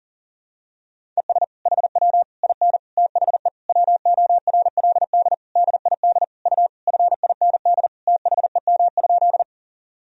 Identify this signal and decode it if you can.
Morse code, heard as a keyed tone, about 40 wpm, 700 Hz. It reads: ES HW IN THE WORLD DID U FIND THEM?
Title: Sentences from Top 200 Words and Abbreviations 40wpm